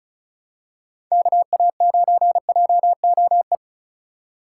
Morse code, heard as a keyed tone, about 36 wpm, 700 Hz. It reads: KA9JOE
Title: USA Call Signs - 6 Characters - 2x Character Spacing - 35wpm